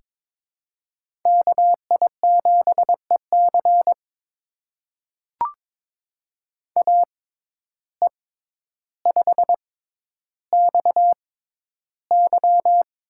Morse code, heard as a keyed tone, about 21 wpm, spaced as for 11 wpm, 700 Hz. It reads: KI7EC AE5XY
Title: USA Call Signs - 5 Characters - 6x Character Spacing - 22wpm